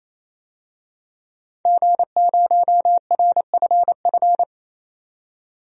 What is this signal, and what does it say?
Morse code, heard as a keyed tone, about 28 wpm, 700 Hz. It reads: G0RFF
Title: DXCC-100 Call Signs - 5 Characters - 28wpm